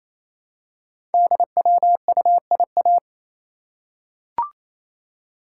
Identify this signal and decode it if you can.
Morse code, heard as a keyed tone, about 27 wpm, 700 Hz. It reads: DWUIA
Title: Code Groups - 5 Letter:Numbers - 28wpm